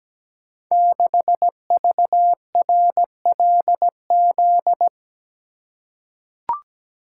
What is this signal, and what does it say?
Morse code, heard as a keyed tone, about 17 wpm, 700 Hz. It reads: 6VRLZ
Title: Code Groups - 5 Letter:Numbers - 17wpm